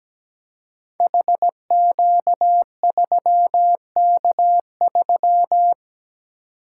Morse code, heard as a keyed tone, about 17 wpm, 700 Hz. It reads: HQ3K3
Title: Code Groups - 5 Letter:Numbers - 17wpm